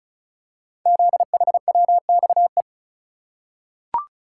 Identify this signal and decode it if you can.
Morse code, heard as a keyed tone, about 36 wpm, 690 Hz. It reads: ZHWXE
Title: Code Groups - 5 Letters - 35wpm